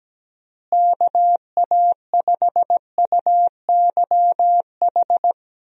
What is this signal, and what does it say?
Morse code, heard as a keyed tone, about 17 wpm, 700 Hz. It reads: KA5UYH
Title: USA Call Signs - 6 Characters - Repeated 3x - 17wpm